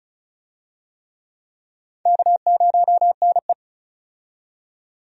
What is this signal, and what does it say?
Morse code, heard as a keyed tone, about 34 wpm, 700 Hz. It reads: K0NE
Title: USA Call Signs - 4 Characters - 35wpm